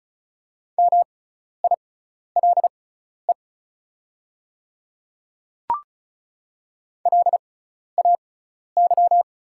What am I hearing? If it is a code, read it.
Morse code, heard as a keyed tone, about 36 wpm, spaced as for 12 wpm, 690 Hz. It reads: MILE LAY
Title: 6x Character Spacing - Top 500 Words - 35wpm